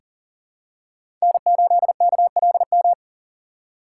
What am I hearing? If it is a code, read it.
Morse code, heard as a keyed tone, about 40 wpm, 700 Hz. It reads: N8KLM
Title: USA Call Signs - 5 Characters - 2x Character Spacing - 40wpm